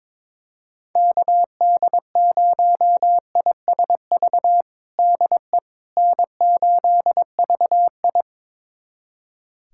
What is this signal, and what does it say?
Morse code, heard as a keyed tone, about 22 wpm, 700 Hz. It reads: KD0ISV DE N8VI
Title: USA Callsign DE Callsign 22wpm